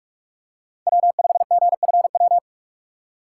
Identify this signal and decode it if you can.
Morse code, heard as a keyed tone, about 42 wpm, 700 Hz. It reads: W5GFW
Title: USA Call Signs - 5 Characters - 3x Character Spacing - 45wpm